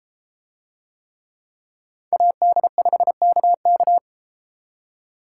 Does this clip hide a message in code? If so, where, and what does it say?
Yes: Morse code from 2.1 to 4.0 s, heard as a keyed tone, about 30 wpm, 700 Hz. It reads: AD5KK